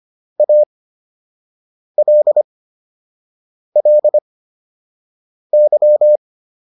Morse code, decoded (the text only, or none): ALLY